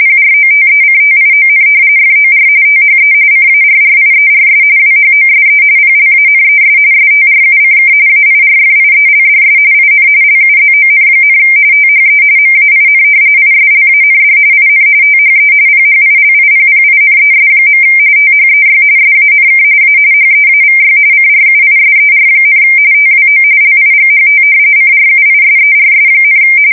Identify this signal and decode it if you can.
RTTY, about 45.45 baud. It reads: PROTECTION FOR THE CARRIERS WHOSE PLANES STRUCK AIRFIELDS IN LUZON, THE MAIN ISLAND IN THE PHILIPPINES. THEY SANK A HEAVY CRUISER, TWO LIGHT CRUISERS,
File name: fsk170 / battleshipnj